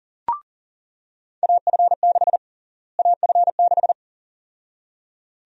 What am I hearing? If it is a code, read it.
Morse code, heard as a keyed tone, about 40 wpm, 700 Hz. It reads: AF6 AF6
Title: USA Call Sign Prefixes - Repeated 2x - 40wpm